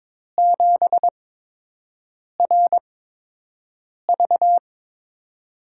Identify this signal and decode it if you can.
Morse code, heard as a keyed tone, about 22 wpm, spaced as for 6 wpm, 700 Hz. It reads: 7RV
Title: USA Call Signs - 5 Characters - 8x Character Spacing - 22wpm